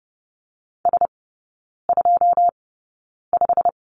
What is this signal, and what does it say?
Morse code, heard as a keyed tone, about 30 wpm, spaced as for 9 wpm, 700 Hz. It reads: S25
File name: course / 30:10wpm / 087 Lesson 15e - Number 5 - Three Characters 30wpm